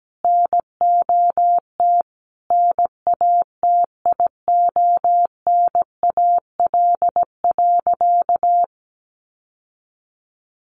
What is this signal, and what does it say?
Morse code, heard as a keyed tone, about 17 wpm, 700 Hz. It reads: NOT NATIONAL.
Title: Sentences from Top 1200 Words 17wpm